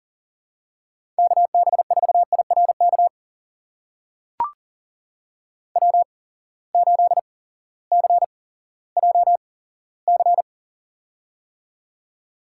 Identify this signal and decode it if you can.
Morse code, heard as a keyed tone, about 40 wpm, spaced as for 18 wpm, 700 Hz. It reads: KB4IRK W8CJC
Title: USA Call Signs - 8x Character Spacing - 40wpm